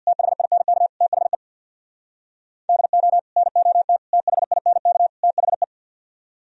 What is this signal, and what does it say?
Morse code, heard as a keyed tone, about 64 wpm, 700 Hz. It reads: THINK THE DO NOT THINK THE